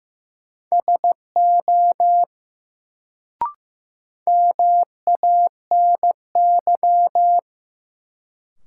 Morse code, heard as a keyed tone, about 15 wpm, 700 Hz. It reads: SO MANY
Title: Sets of 2 Words - New Word Drill - 15wpm